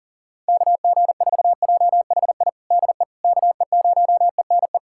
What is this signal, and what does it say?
Morse code, heard as a keyed tone, about 40 wpm, 700 Hz. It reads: KG4JHI DE KE0ENE
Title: USA Callsign DE Callsign 40wpm